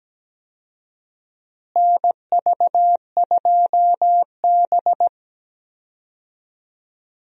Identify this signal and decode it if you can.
Morse code, heard as a keyed tone, about 17 wpm, 700 Hz. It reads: NV2B
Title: USA Call Signs - 4 Characters - Repeated 2x - 17wpm